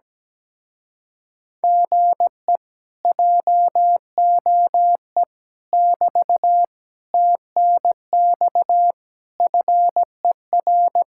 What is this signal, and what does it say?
Morse code, heard as a keyed tone, about 17 wpm, 700 Hz. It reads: GE JOE <BT> TNX FER
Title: Speed-Racing Common QSO Phrases - 17wpm